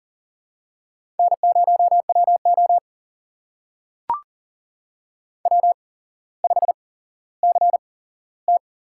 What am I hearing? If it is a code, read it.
Morse code, heard as a keyed tone, about 40 wpm, spaced as for 17 wpm, 700 Hz. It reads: N0WO W5CT
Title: USA Call Signs - 4 Characters - 8x Character Spacing - 40wpm